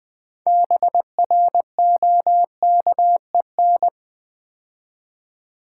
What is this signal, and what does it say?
Morse code, heard as a keyed tone, about 20 wpm, 700 Hz. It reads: BROKEN
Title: 6 Letter Words - First 3 Letters Given - 20wpm